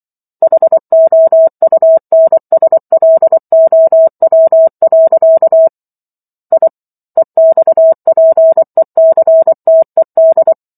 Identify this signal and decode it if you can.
Morse code, heard as a keyed tone, about 24 wpm, 650 Hz. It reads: HOUNSLOW. I EXPECTED